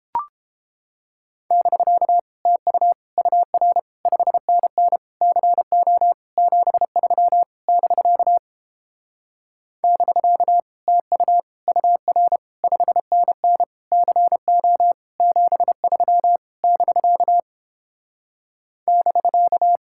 Morse code, heard as a keyed tone, about 29 wpm, 700 Hz. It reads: <BK> TU UR 5NN CO 73 <BK> <BK> TU UR 5NN CO 73 <BK> <BK>